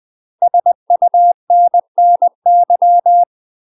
Morse code, heard as a keyed tone, about 20 wpm, 700 Hz. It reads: SUNNY
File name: UK MASTER QSOs TRAINING/20WPM